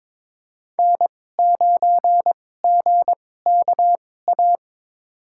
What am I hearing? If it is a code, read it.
Morse code, heard as a keyed tone, about 22 wpm, spaced as for 16 wpm, 700 Hz. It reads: N9GKA
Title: USA Call Signs - 5 Characters - 2x Character Spacing - 22wpm